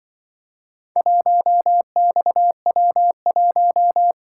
Morse code, heard as a keyed tone, about 23 wpm, 700 Hz. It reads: 1XW1
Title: Speed-Racing - Four Letter-Number Combinations - 20wpm